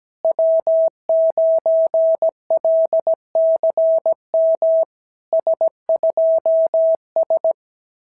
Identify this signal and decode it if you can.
Morse code, heard as a keyed tone, about 17 wpm, 650 Hz. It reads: W9LCM S2S
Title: SOTA QSO Practice 17wpm